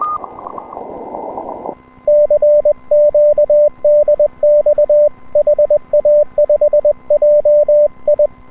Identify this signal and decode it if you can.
Morse code, heard as a keyed tone, about 21 wpm, 590 Hz. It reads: CQ DX HA5JI
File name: n6tr / N6TR-7deHA5JIon160M-06-Sep- 31UTC